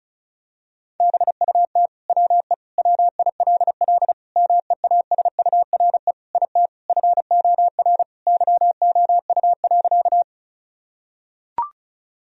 Morse code, heard as a keyed tone, about 34 wpm, 700 Hz. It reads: BUT WE WILL MEASURE IT FOR YOU.
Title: Sentences from Top 1000 Words - Review Part 2 - 35wpm